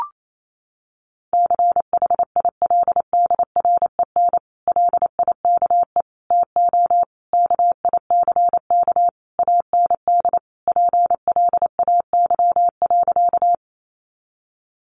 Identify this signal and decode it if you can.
Morse code, heard as a keyed tone, about 28 wpm, 700 Hz. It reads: CHILDREN LIKE TO KICK AND PLAY.
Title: Sentences from Top 1300 Words 28wpm